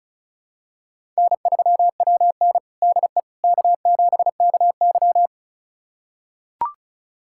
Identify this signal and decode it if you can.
Morse code, heard as a keyed tone, about 36 wpm, 700 Hz. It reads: N3WN DE K7KY
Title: USA Callsign DE Callsign - 4 Characters - 35wpm